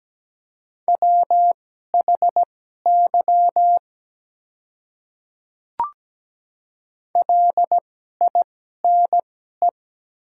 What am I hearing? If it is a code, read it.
Morse code, heard as a keyed tone, about 17 wpm, spaced as for 12 wpm, 700 Hz. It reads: WHY LINE
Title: 2x Character Spacing - Top 300 Words - 17wpm